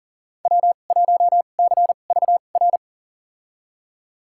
Morse code, heard as a keyed tone, about 40 wpm, spaced as for 29 wpm, 700 Hz. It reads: W1CVR
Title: USA Call Signs - 5 Characters - 2x Character Spacing - 40wpm